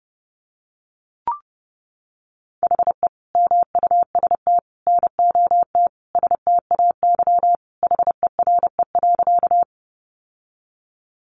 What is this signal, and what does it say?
Morse code, heard as a keyed tone, about 28 wpm, 700 Hz. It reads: HE MUST NOT STAY HERE.
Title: Sentences from Top 600 Words 30wpm